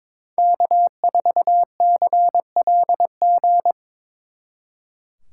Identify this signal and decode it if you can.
Morse code, heard as a keyed tone, about 25 wpm, 700 Hz. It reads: K4CLG
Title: Speed Racing - USA Call Signs 22wpm